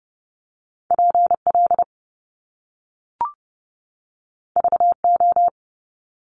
Morse code, heard as a keyed tone, about 30 wpm, 700 Hz. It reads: PL VO
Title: Top Two Letter Combinations in Words 30wpm